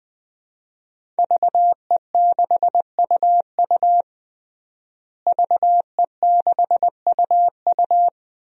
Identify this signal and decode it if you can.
Morse code, heard as a keyed tone, about 20 wpm, 700 Hz. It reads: VE6UU VE6UU
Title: DXCC-100 Call Signs - 5 Characters - Repeated 2x - 20wpm